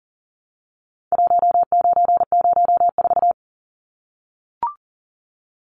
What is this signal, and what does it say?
Morse code, heard as a keyed tone, about 40 wpm, 700 Hz. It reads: 1904
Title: Numbers - 4-Digit 40wpm